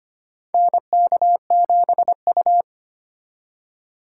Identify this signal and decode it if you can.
Morse code, heard as a keyed tone, about 25 wpm, 700 Hz. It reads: NK7U
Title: K1USN SST QSO Practice - 25wpm